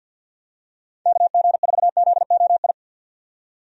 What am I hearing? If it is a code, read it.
Morse code, heard as a keyed tone, about 49 wpm, 700 Hz. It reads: KG4ZOI
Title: USA Call Signs - 6 Characters - 50wpm